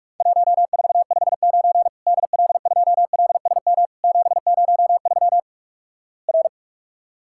Morse code, heard as a keyed tone, about 47 wpm, 700 Hz. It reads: 1459 DL2LSM 703 R
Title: Worked All Europe DX Contest - QTC Exchange - 45wpm